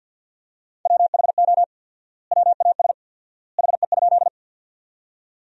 Morse code, heard as a keyed tone, about 50 wpm, 700 Hz. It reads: WHO WAS HE?
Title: 4x Word-Spacing - Sentences from Top 100 Words - 50wpm